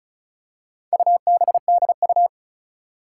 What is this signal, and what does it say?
Morse code, heard as a keyed tone, about 36 wpm, 700 Hz. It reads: UBDU
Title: Four Challenging Letter-Number Combinations 35wpm